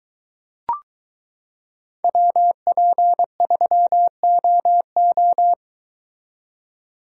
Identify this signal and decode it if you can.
Morse code, heard as a keyed tone, about 23 wpm, 700 Hz. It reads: WP3OO